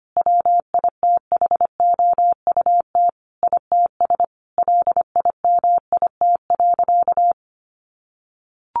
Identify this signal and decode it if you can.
Morse code, heard as a keyed tone, about 25 wpm, 700 Hz. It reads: WITHOUT ITS LIMIT.
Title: Sentences from Top 1500 Words 25wpm